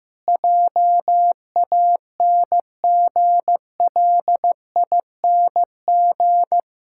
Morse code, heard as a keyed tone, about 16 wpm, 700 Hz. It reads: JANGLING
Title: Pangram Challenge v2 - 15wpm